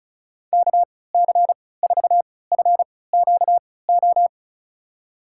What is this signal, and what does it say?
Morse code, heard as a keyed tone, about 34 wpm, spaced as for 20 wpm, 700 Hz. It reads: KC4FQO
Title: USA Call Signs - 3x Character Spacing - 35wpm